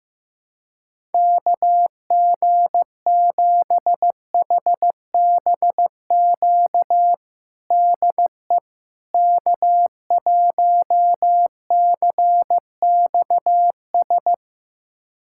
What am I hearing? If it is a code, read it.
Morse code, heard as a keyed tone, about 15 wpm, 700 Hz. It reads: KG7HBQ DE K1CXS
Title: USA Callsign DE Callsign 15wpm